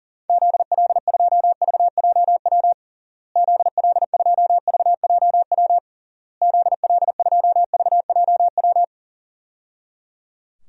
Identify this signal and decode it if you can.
Morse code, heard as a keyed tone, about 37 wpm, 700 Hz. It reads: ZL2VJW ZL2VJW ZL2VJW
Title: DXCC-100 Call Signs - 6 Characters - Repeated 3x - 40wpm